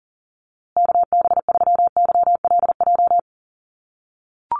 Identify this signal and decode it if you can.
Morse code, heard as a keyed tone, about 40 wpm, 700 Hz. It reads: KB3YLJ
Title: USA Call Signs Repeated 3x 40wpm